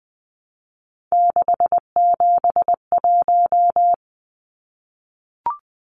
Morse code, heard as a keyed tone, about 20 wpm, 700 Hz. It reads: 671
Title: Numbers - 3-Digit 20wpm